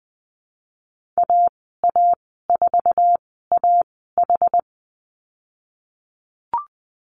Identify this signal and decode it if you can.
Morse code, heard as a keyed tone, about 20 wpm, spaced as for 14 wpm, 700 Hz. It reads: AA4AH
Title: 076 Lesson 13g - Letter L - Call Signs 20wpm